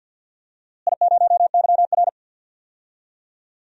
Morse code, heard as a keyed tone, about 52 wpm, 700 Hz. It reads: I0YR
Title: DXCC-100 Call Signs - 4 Characters - Repeated 3x - 50wpm